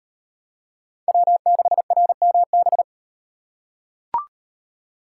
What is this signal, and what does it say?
Morse code, heard as a keyed tone, about 37 wpm, 700 Hz. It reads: W6RMB